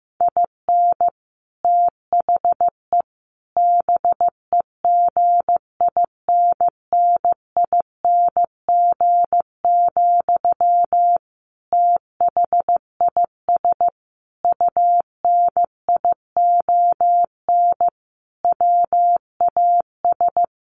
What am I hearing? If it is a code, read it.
Morse code, heard as a keyed tone, about 14 wpm, 700 Hz. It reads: IN THE BEGINNING, THIS UNION WAS
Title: Sentences from Top 1400 Words 15wpm